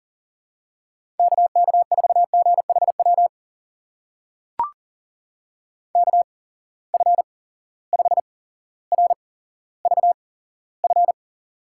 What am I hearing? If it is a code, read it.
Morse code, heard as a keyed tone, about 40 wpm, spaced as for 17 wpm, 700 Hz. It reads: KK4GHW KF5RVF
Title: USA Call Signs - 6 Characters - 8x Character Spacing - 40wpm